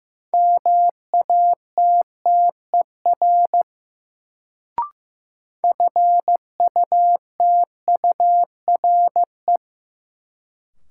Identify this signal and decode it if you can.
Morse code, heard as a keyed tone, about 16 wpm, 700 Hz. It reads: MATTER FUTURE
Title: Sentences from Top 700 Words - New Word Drill - 15wpm